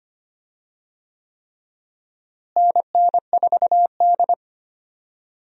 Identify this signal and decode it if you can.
Morse code, heard as a keyed tone, about 25 wpm, 700 Hz. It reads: NN4D
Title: USA Call Signs - 4 Characters - Repeated 3x - 25wpm